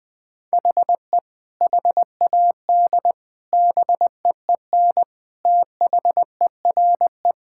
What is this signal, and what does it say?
Morse code, heard as a keyed tone, about 20 wpm, 700 Hz. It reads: HE HAD BEEN THERE
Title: Sentences with 3-Word Phrases - Spoken Beginning - 20wpm